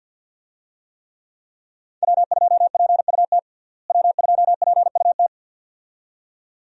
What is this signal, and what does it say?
Morse code, heard as a keyed tone, about 52 wpm, 700 Hz. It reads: W2PUT W2PUT
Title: USA Call Signs - 5 Characters - Repeated 2x - 50wpm